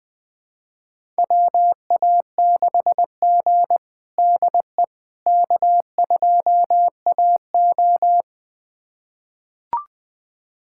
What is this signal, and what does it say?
Morse code, heard as a keyed tone, about 20 wpm, 700 Hz. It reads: WA6G DE K2AO
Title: USA Callsign DE Callsign - 4 Characters - 20wpm